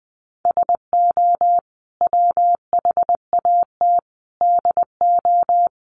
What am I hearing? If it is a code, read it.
Morse code, heard as a keyed tone, about 20 wpm, 700 Hz. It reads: SO WHAT DO
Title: Sets of 5 Words Encore2 20wpm